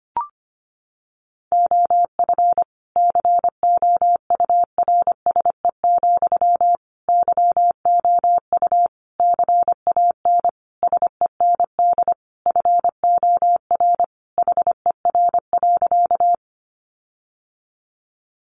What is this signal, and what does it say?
Morse code, heard as a keyed tone, about 25 wpm, 700 Hz. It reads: OF COURSE, YOU CAN SEND FOR HER.
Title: Sentences from Top 1500 Words 25wpm